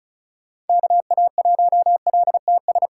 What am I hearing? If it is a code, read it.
Morse code, heard as a keyed tone, about 36 wpm, 700 Hz. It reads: KA1LTS
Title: USA Call Signs - 6 Characters - 3x Character Spacing - 35wpm